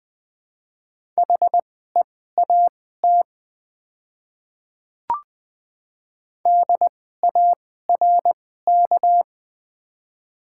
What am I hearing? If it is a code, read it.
Morse code, heard as a keyed tone, about 20 wpm, spaced as for 14 wpm, 700 Hz. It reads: HEAT DARK
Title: 2x Character Spacing - Top 500 Words - 20wpm